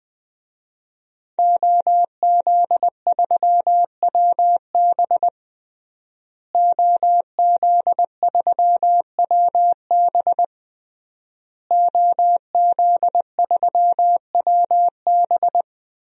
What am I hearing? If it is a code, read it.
Morse code, heard as a keyed tone, about 20 wpm, 700 Hz. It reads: OZ3WB OZ3WB OZ3WB
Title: DXCC-100 Call Signs - 5 Characters - Repeated 3x - 20wpm